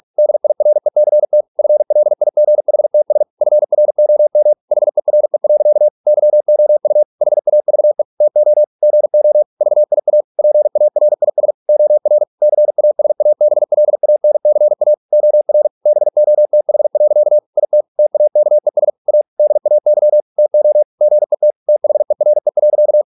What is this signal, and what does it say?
Morse code, heard as a keyed tone, about 46 wpm, 600 Hz. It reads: DIRECT FLIGHTS FROM HERE. YOU HAVE TO GO VIA PARIS OR CASABLANCA OR BOTH. IT TAKES A DAY TO GET THERE.